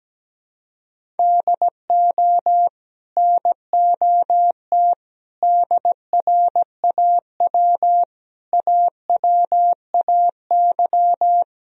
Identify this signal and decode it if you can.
Morse code, heard as a keyed tone, about 17 wpm, 700 Hz. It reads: DO NOT DRAW AWAY
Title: Sentences from Top 1000 Words - Review Part 1 - 17wpm